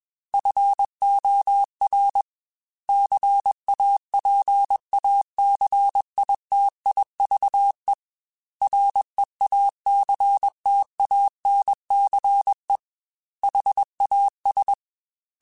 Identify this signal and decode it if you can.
Morse code, heard as a keyed tone, about 21 wpm, 800 Hz. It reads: FOR CAPACITIVE REACTANCE HAS
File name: Technical Subjects / Technical Subjects-mp3s / XC @20